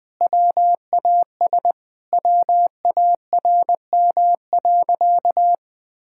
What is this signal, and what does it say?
Morse code, heard as a keyed tone, about 21 wpm, 700 Hz. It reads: WAS WARM.
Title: Sentences from Top 1000 Words - Review Part 5 - 20wpm